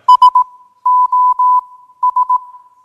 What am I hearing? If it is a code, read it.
Morse code, heard as a keyed tone, about 18 wpm, 1000 Hz. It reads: S O S